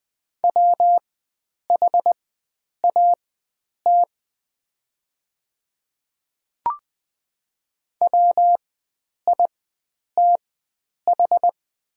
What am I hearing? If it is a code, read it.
Morse code, heard as a keyed tone, about 20 wpm, spaced as for 9 wpm, 700 Hz. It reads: WHAT WITH
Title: 4x Character Spacing - Top 100 Words - 20wpm